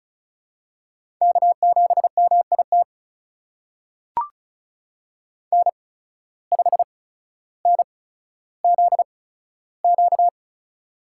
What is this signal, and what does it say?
Morse code, heard as a keyed tone, about 36 wpm, spaced as for 15 wpm, 700 Hz. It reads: K7MIT N5NZQ
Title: USA Call Signs - 5 Characters - 8x Character Spacing - 35wpm